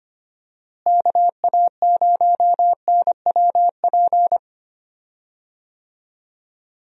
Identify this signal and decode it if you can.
Morse code, heard as a keyed tone, about 25 wpm, 700 Hz. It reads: KA0NWP
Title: USA Call Signs - 6 Characters - Repeated 2x - 25wpm